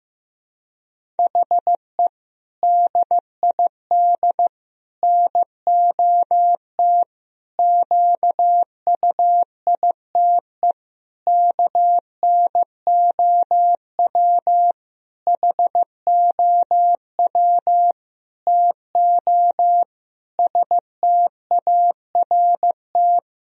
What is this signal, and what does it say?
Morse code, heard as a keyed tone, about 16 wpm, 700 Hz. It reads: HE DID NOT QUITE KNOW HOW TO START
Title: Sentences from Top 1000 Words - Review Part 1 - 15wpm